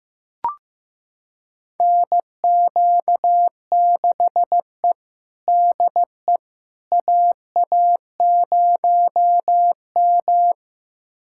Morse code, heard as a keyed tone, about 16 wpm, 700 Hz. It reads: NQ6E DE AA0M
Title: USA Callsign DE Callsign - 4 Characters - 15wpm